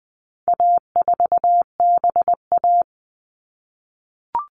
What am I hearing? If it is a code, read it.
Morse code, heard as a keyed tone, about 20 wpm, 700 Hz. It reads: A4BA
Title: All Cut Numbers - 4-Digit 20wpm